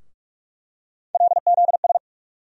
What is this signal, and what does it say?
Morse code, heard as a keyed tone, about 42 wpm, 700 Hz. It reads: LZS
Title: USA Call Sign Postfixes - 3 Letters 45wpm